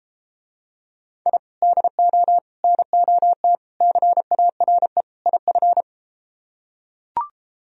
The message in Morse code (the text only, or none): I DO NOT CARE IF